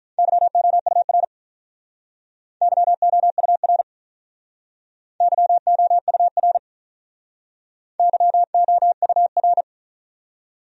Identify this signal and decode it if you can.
Morse code, heard as a keyed tone, about 44 wpm, 700 Hz. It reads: YOUR YOUR YOUR YOUR